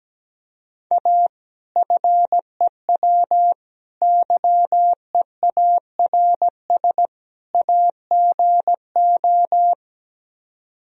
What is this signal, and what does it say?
Morse code, heard as a keyed tone, about 17 wpm, 700 Hz. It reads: A FEW YEARS AGO
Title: Speed-Racing - Sets of 4 Words 17wpm